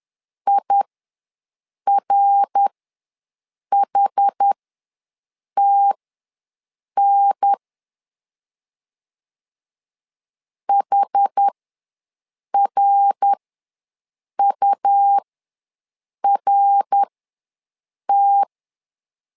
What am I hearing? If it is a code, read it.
Morse code, heard as a keyed tone, about 11 wpm, spaced as for 6 wpm, 800 Hz. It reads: IRHTN HRURT
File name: LEZIONI / 3 MWRO / 3 MWR ws 10wpm 2151